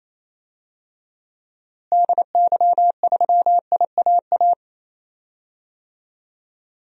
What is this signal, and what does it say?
Morse code, heard as a keyed tone, about 28 wpm, 700 Hz. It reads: DY3IAA
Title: DXCC-100 Call Signs - 6 Characters - Repeated 3x - 28wpm